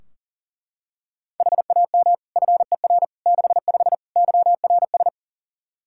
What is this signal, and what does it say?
Morse code, heard as a keyed tone, about 40 wpm, 700 Hz. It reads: HAM FER 65 YRS
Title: Practice QSO - Exchange 3 - More Info - 40wpm